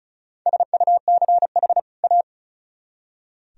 Morse code, heard as a keyed tone, about 45 wpm, 700 Hz. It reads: SUCH A